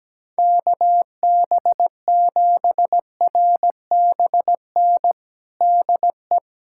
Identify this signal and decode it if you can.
Morse code, heard as a keyed tone, about 17 wpm, 700 Hz. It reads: KB7RBN DE
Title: USA Callsign DE Callsign 17wpm